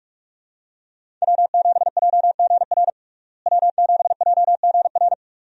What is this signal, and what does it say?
Morse code, heard as a keyed tone, about 42 wpm, 700 Hz. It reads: W7JGR W7JGR
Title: USA Call Signs - 5 Characters - Repeated 3x - 45wpm